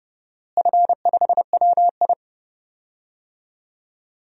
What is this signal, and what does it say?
Morse code, heard as a keyed tone, about 30 wpm, 700 Hz. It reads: F5WI
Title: DXCC-100 Call Signs - 4 Characters - Repeated 3x - 30wpm